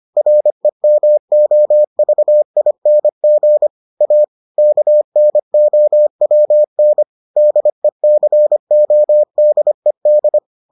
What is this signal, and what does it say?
Morse code, heard as a keyed tone, about 25 wpm, 600 Hz. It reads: REMOVING A KNOWN DECODED